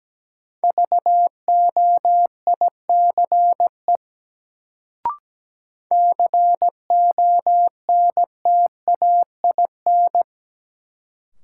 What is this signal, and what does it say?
Morse code, heard as a keyed tone, about 17 wpm, 700 Hz. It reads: VOICE CONTAIN
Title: Sentences from Top 500 Words - New Word Drill - 17wpm